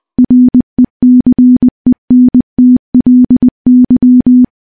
Morse code, heard as a keyed tone, about 21 wpm, 250 Hz. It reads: RECENTLY
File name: morse-training-20WPM